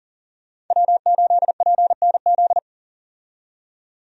Morse code, heard as a keyed tone, about 40 wpm, 700 Hz. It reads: W8PNZ
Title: USA Call Signs - 5 Characters - 40wpm